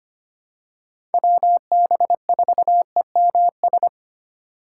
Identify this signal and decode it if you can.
Morse code, heard as a keyed tone, about 25 wpm, 700 Hz. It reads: WB4EMS
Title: USA Call Signs - 2x Character Spacing - 25wpm